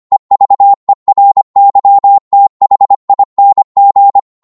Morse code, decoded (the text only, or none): EVERYTHING